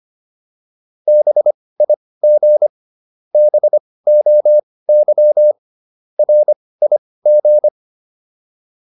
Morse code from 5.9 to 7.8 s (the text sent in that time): RIG